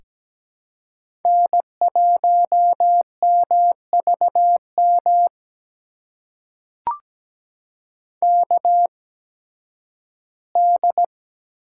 Morse code, heard as a keyed tone, about 17 wpm, spaced as for 11 wpm, 700 Hz. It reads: N1MVM KD
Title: USA Call Signs - 8x Character Spacing - 17wpm